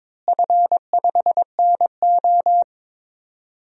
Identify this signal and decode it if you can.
Morse code, heard as a keyed tone, about 22 wpm, 700 Hz. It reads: F5NO
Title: DXCC-100 Call Signs - 4 Characters - Repeated 2x - 22wpm